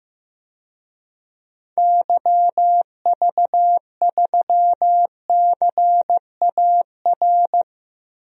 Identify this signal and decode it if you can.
Morse code, heard as a keyed tone, about 15 wpm, 700 Hz. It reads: YV3CAR
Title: DXCC-100 Call Signs - 6 Characters - Repeated 2x - 15wpm